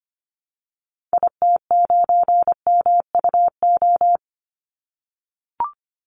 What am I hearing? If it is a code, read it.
Morse code, heard as a keyed tone, about 25 wpm, 700 Hz. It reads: IT9MUO
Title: Call Signs - 25wpm